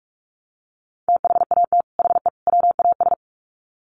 Morse code, heard as a keyed tone, about 44 wpm, 700 Hz. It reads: THAT HE WAS
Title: Sets of 3 Words 45wpm